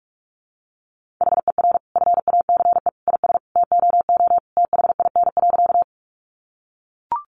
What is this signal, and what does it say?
Morse code, heard as a keyed tone, about 44 wpm, 700 Hz. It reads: HER FACE IS TOO THIN.
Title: Sentences from Top 1200 Words 45wpm